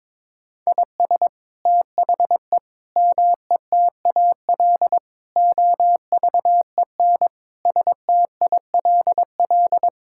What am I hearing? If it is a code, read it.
Morse code, heard as a keyed tone, about 22 wpm, 700 Hz. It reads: IS THE METAL OVEN STILL
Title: Sentences from Top 1000 Words - Review Part 6 - 22wpm